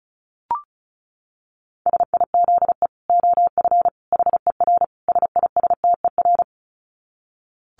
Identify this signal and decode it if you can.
Morse code, heard as a keyed tone, about 34 wpm, 700 Hz. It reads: SIZE OF HER SISTER
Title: Dit Challenge 35wpm